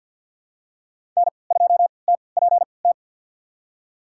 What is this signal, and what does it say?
Morse code, heard as a keyed tone, about 52 wpm, spaced as for 29 wpm, 700 Hz. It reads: N2TPT
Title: USA Call Signs - 5 Characters - 3x Character Spacing - 50wpm